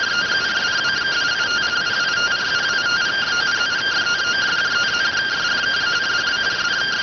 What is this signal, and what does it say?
Two-tone teleprinter signal, about 170 Hz shift.